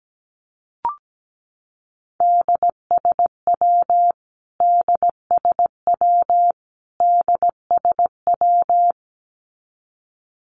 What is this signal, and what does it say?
Morse code, heard as a keyed tone, about 17 wpm, 700 Hz. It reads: DSW DSW DSW
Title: Abbreviations Repeated 3x - 17wpm